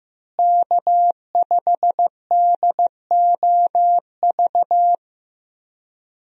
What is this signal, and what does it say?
Morse code, heard as a keyed tone, about 15 wpm, 700 Hz. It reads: K5DOV
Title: USA Call Signs - 5 Characters - Repeated 3x - 15wpm